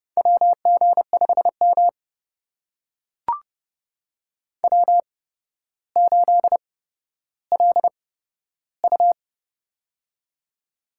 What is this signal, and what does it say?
Morse code, heard as a keyed tone, about 30 wpm, spaced as for 13 wpm, 700 Hz. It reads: WG5M W8LU
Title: USA Call Signs - 4 Characters - 8x Character Spacing - 30wpm